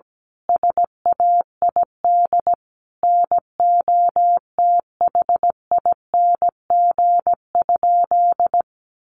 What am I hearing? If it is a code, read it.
Morse code, heard as a keyed tone, about 17 wpm, 700 Hz. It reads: SAID NOTHING?
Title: Sentences from Top 500 Words 17wpm